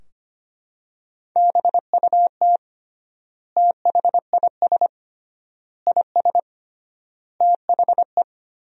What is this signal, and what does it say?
Morse code, heard as a keyed tone, about 25 wpm, 700 Hz. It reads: BUT THIS IS THE
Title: 3x Word-Spacing - Sentences from Top 200 Words - 25wpm